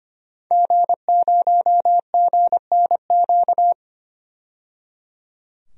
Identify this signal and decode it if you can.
Morse code, heard as a keyed tone, about 25 wpm, 700 Hz. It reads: G0GNQ
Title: DXCC-100 Call Signs - 5 Characters - 25wpm